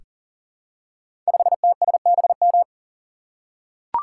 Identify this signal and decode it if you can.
Morse code, heard as a keyed tone, about 42 wpm, 700 Hz. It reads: 5TSBM
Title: Code Groups - 5 Letter:Numbers - 40wpm